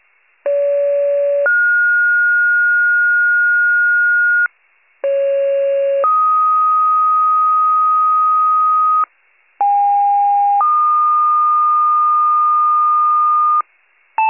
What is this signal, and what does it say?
Two-tone page, heard as a one-second tone, then a three-second tone, about 570 then 1430 Hz, about 550 then 1250 Hz, about 810 then 1220 Hz. It is heard from 0.5 to 13.6 s.